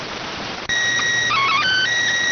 SSTV (slow-scan television), PD 290.